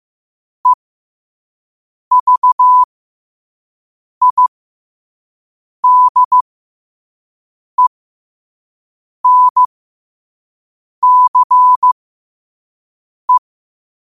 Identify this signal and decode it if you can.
Morse code, heard as a keyed tone, about 15 wpm, spaced as for 5 wpm, 1000 Hz. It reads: EVIDENCE
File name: qotdfiles / 5wpm